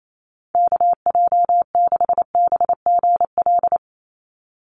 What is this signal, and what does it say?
Morse code, heard as a keyed tone, about 28 wpm, 700 Hz. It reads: KJ6BGL
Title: USA Call Signs 28wpm